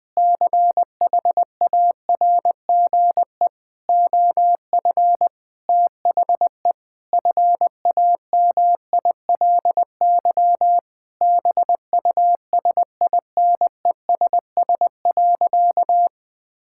Morse code, heard as a keyed tone, about 20 wpm, 700 Hz. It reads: CHARGE OF THE FAMILY BUSINESS.
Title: Sentences from Top 1000 Words - Review Part 6 - 20wpm